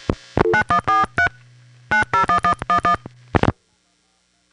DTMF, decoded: 9143 901111